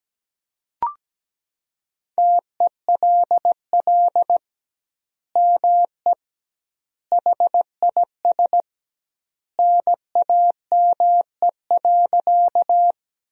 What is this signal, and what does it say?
Morse code, heard as a keyed tone, about 17 wpm, 700 Hz. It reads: TELL ME HIS NAME.
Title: 2x Word-Spacing - Sentences from Top 200 Words - 17wpm